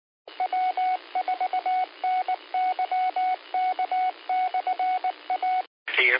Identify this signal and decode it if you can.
Morse code, heard as a keyed tone, about 19 wpm, 740 Hz, with radio noise. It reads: W4NYK/A
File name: qsorec SimplexLogic